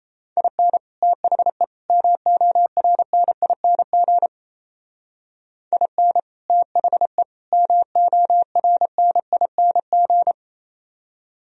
Morse code, heard as a keyed tone, about 33 wpm, 700 Hz. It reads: IN THE MORNING IN THE MORNING
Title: Speed-Racing - Sets of 3 Words 28wpm